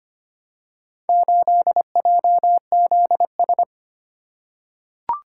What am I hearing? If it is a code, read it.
Morse code, heard as a keyed tone, about 25 wpm, 700 Hz. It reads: 8JZS
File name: practice / 25wpm / Four Letter-Number Combinations - 25wpm